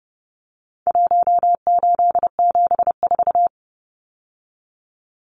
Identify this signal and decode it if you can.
Morse code, heard as a keyed tone, about 30 wpm, 700 Hz. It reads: 1874
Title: Numbers - Contest Style - 4-Digit 30wpm